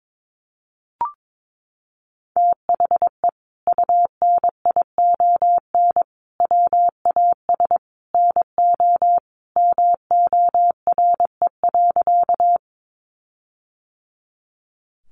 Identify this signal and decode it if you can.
Morse code, heard as a keyed tone, about 22 wpm, 700 Hz. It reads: THE UNION WAS NO MORE.
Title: Sentences from Top 1400 Words 22wpm